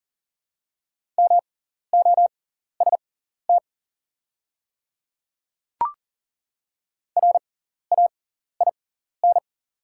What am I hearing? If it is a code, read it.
Morse code, heard as a keyed tone, about 40 wpm, spaced as for 14 wpm, 700 Hz. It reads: MOST RAIN
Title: 6x Character Spacing - Top 500 Words - 40wpm